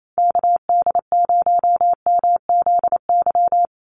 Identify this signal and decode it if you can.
Morse code, heard as a keyed tone, about 28 wpm, 700 Hz. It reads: KD0MZY
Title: USA Call Signs 28wpm